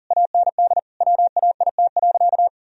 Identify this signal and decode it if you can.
Morse code, heard as a keyed tone, about 37 wpm, 700 Hz. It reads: AND WAIT.
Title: Sentences from Top 1000 Words - Review Part 4 - 40wpm